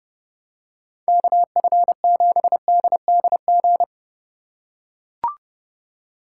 Morse code, heard as a keyed tone, about 30 wpm, 700 Hz. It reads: KF7DDG